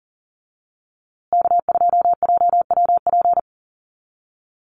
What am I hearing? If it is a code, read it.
Morse code, heard as a keyed tone, about 40 wpm, 700 Hz. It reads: K2JWP
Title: USA Call Signs Repeated 3x 40wpm